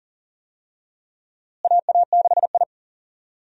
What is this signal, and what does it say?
Morse code, heard as a keyed tone, about 38 wpm, 700 Hz. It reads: AA6I
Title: USA Call Signs - 4 Characters - Repeated 3x - 40wpm